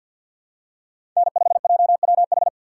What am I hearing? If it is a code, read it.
Morse code, heard as a keyed tone, about 47 wpm, 700 Hz. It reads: N5JWH
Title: USA Call Signs - 5 Characters - 6x Character Spacing - 50wpm